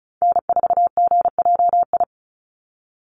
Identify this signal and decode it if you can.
Morse code, heard as a keyed tone, about 34 wpm, 700 Hz. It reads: N4GJI
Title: USA Call Signs 35wpm